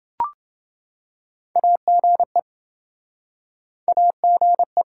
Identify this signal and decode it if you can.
Morse code, heard as a keyed tone, about 30 wpm, 700 Hz. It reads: AGE AGE